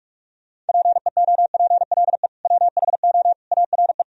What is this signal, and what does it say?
Morse code, heard as a keyed tone, about 57 wpm, 700 Hz. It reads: PEOPLE WHO ARE